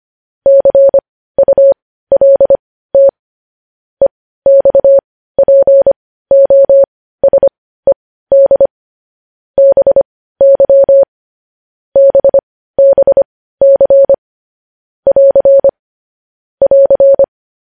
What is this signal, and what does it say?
Morse code, heard as a keyed tone, about 25 wpm, spaced as for 15 wpm, 550 Hz. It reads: CULT EXPOSED BY BBC <AR> <AR>
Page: news-headlines-in-morse-code-at-15-wpm